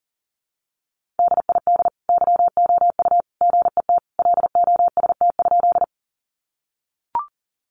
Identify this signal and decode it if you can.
Morse code, heard as a keyed tone, about 40 wpm, 700 Hz. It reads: DID YOU GET LOST?
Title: Sentences from Top 1400 Words 40wpm